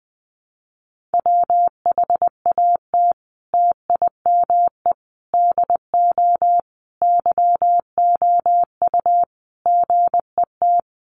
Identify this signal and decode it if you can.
Morse code, heard as a keyed tone, about 19 wpm, 700 Hz. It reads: WHAT TIME DO YOU GET
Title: Sentences from Top 1500 Words 20wpm